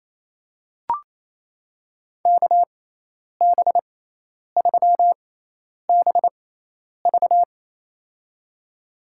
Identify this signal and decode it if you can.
Morse code, heard as a keyed tone, about 28 wpm, spaced as for 10 wpm, 700 Hz. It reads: KB3BV
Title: USA Call Signs - 6x Character Spacing - 28wpm